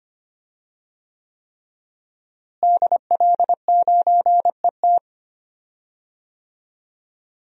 Morse code, heard as a keyed tone, about 25 wpm, 700 Hz. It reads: DL9ET
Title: DXCC-100 Call Signs - 5 Characters - Repeated 3x - 25wpm